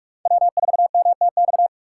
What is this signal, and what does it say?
Morse code, heard as a keyed tone, about 44 wpm, 700 Hz. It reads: W4MTX